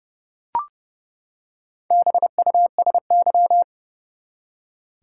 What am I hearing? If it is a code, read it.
Morse code, heard as a keyed tone, about 31 wpm, 700 Hz. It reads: BUSY
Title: Four Letter Words - 30wpm